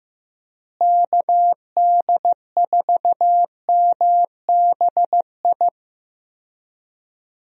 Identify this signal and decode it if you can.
Morse code, heard as a keyed tone, about 15 wpm, 700 Hz. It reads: KD4MBI
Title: USA Call Signs - 6 Characters - Repeated 2x - 15wpm